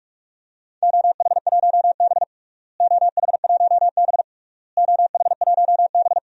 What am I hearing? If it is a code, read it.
Morse code, heard as a keyed tone, about 41 wpm, 700 Hz. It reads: OH1B OH1B OH1B
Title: DXCC-100 Call Signs - 4 Characters - Repeated 3x - 45wpm